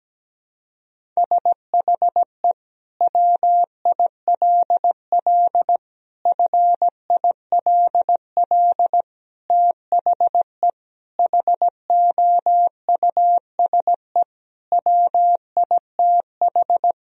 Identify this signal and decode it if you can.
Morse code, heard as a keyed tone, about 17 wpm, 700 Hz. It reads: SHE WILL FILL THE HOUSE WITH
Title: Sentences from Top 1000 Words - Review Part 3 - 17wpm